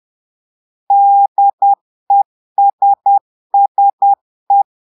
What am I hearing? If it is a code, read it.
Morse code, heard as a keyed tone, about 10 wpm, 800 Hz. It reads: DESSE
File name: portugais texte en clair 2 vitesse 10 mots minutes